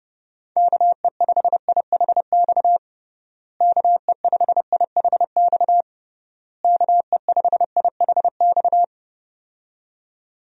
Morse code, heard as a keyed tone, about 28 wpm, 700 Hz. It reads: KE5IHX KE5IHX KE5IHX
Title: USA Call Signs - 6 Characters - Repeated 3x - 30wpm